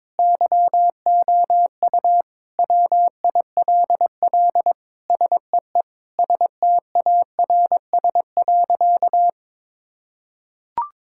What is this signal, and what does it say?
Morse code, heard as a keyed tone, about 22 wpm, 700 Hz. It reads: YOU WILL SEE STARS.
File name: practice / 22wpm / Sentences with 4-Word Phrases - 22wpm